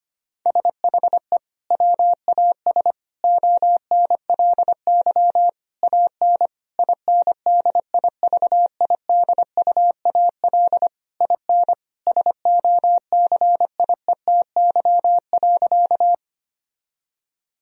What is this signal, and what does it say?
Morse code, heard as a keyed tone, about 25 wpm, 700 Hz. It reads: SHE WAS ONLY AN INDIVIDUAL IN SOCIETY.
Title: Sentences from Top 1000 Words - Review Part 4 - 25wpm